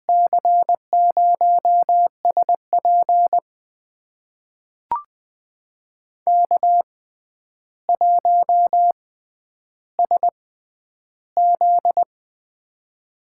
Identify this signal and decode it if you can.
Morse code, heard as a keyed tone, about 20 wpm, spaced as for 10 wpm, 700 Hz. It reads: C0SP K1SZ
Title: USA Call Signs - 6x Character Spacing - 20wpm